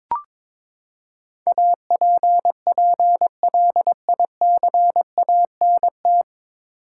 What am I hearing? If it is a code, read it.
Morse code, heard as a keyed tone, about 21 wpm, 700 Hz. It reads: APPLICANT
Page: Morse Code Ninja practice 22wpm